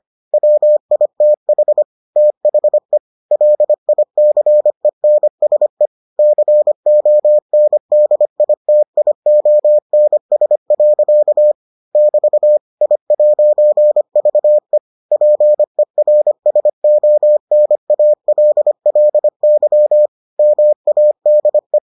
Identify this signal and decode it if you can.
Morse code, heard as a keyed tone, about 25 wpm, 600 Hz. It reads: WITH THE LICENSE CONDITIONS. <BT> I'VE PERSONALLY MADE